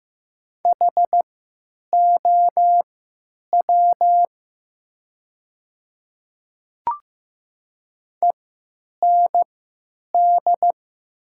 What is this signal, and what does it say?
Morse code, heard as a keyed tone, about 15 wpm, spaced as for 9 wpm, 700 Hz. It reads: HOW END
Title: 3x Character Spacing - Top 200 Words - 15wpm